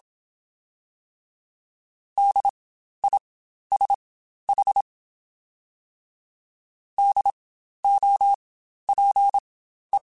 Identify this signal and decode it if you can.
Morse code, heard as a keyed tone, about 27 wpm, spaced as for 12 wpm, 800 Hz. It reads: DISH DOPE
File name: Spaced Alphas / Spaced Alphas mp3s / Ds sp @25